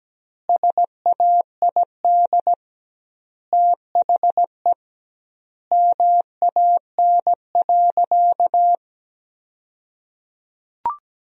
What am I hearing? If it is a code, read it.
Morse code, heard as a keyed tone, about 17 wpm, 700 Hz. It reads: SAID THE MAN.
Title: 2x Word-Spacing - Sentences from Top 200 Words - 17wpm